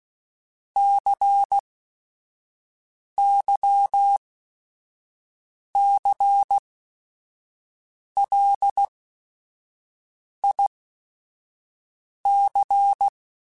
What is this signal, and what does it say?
Morse code, heard as a keyed tone, about 16 wpm, spaced as for 5 wpm, 800 Hz. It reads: CYCLIC
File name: Technical Subjects / Technical Subjects-mp3s / XC @5